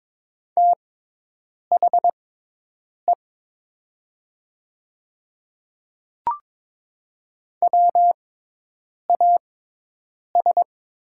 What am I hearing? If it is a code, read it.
Morse code, heard as a keyed tone, about 22 wpm, spaced as for 8 wpm, 700 Hz. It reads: THE WAS
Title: 6x Character Spacing - Top 100 Words - 22wpm